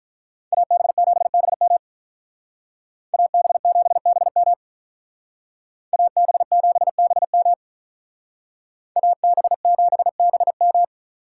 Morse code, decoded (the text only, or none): AB7BM AB7BM AB7BM AB7BM